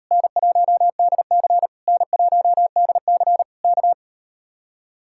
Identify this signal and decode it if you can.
Morse code, heard as a keyed tone, about 39 wpm, 700 Hz. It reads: N1DC N1DC K